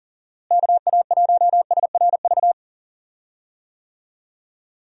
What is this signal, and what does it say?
Morse code, heard as a keyed tone, about 40 wpm, 700 Hz. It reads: KA1SRV
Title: USA Call Signs - 6 Characters - Repeated 3x - 40wpm